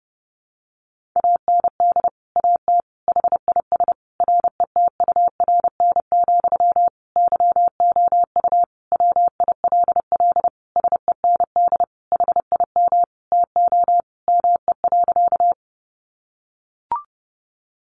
Morse code, heard as a keyed tone, about 30 wpm, 700 Hz. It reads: AND AT HIS RETURN, YOU WILL SEND HIM TO ME.
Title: Sentences from Top 1500 Words 30wpm